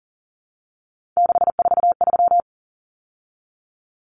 Morse code, heard as a keyed tone, about 40 wpm, 700 Hz. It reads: 643
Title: Common Cut Numbers - 3-Digit 40wpm